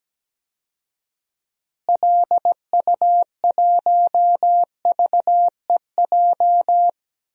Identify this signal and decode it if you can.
Morse code, heard as a keyed tone, about 17 wpm, 700 Hz. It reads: LU1VEJ
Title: DXCC-100 Call Signs - 6 Characters - Repeated 3x - 17wpm